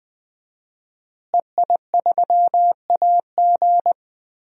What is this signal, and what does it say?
Morse code, heard as a keyed tone, about 20 wpm, 700 Hz. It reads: EI3AG
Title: DXCC-100 Call Signs - 5 Characters - Repeated 3x - 20wpm